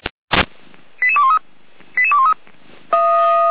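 Five-tone selective call (ZVEI), heard as quick runs of five tones, tones of about 70 ms. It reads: 88213 88213
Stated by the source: ZVEI five-tone selective call 88124 received with an SDR, archived by Feuerwehr Orlamünde (fire brigade).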